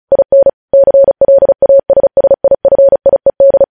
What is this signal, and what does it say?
Morse code, heard as a keyed tone, about 36 wpm, 550 Hz. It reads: IN CLASSIFIED
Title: News Headlines at 35